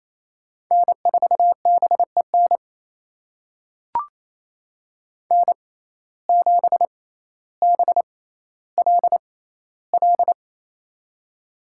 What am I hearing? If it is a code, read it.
Morse code, heard as a keyed tone, about 28 wpm, spaced as for 14 wpm, 700 Hz. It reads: N4BEN N7BLL
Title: USA Call Signs - 5 Characters - 6x Character Spacing - 28wpm